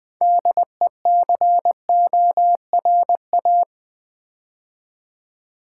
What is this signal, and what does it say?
Morse code, heard as a keyed tone, about 21 wpm, 700 Hz. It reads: DECORA
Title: Build A Word - 2 Letters At A Time - 8-Letter Words - 20wpm